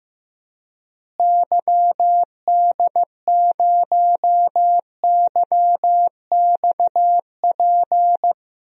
Morse code, heard as a keyed tone, about 15 wpm, 700 Hz. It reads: YD0YXP
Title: DXCC-100 Call Signs - 6 Characters - Repeated 2x - 15wpm